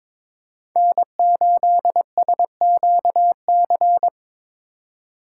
Morse code, heard as a keyed tone, about 21 wpm, 700 Hz. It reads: N8SQC
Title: USA Call Signs - 2x Character Spacing - 22wpm